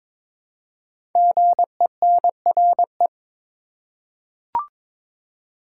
Morse code, heard as a keyed tone, about 22 wpm, 700 Hz. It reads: GENRE
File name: practice / 22wpm / 5 Letter Words - First 3 Letters Given - 22wpm